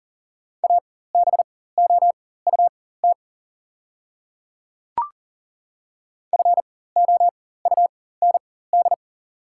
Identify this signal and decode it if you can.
Morse code, heard as a keyed tone, about 40 wpm, spaced as for 19 wpm, 690 Hz. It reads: ABOUT FOUND
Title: 4x Character Spacing - Top 300 Words - 40wpm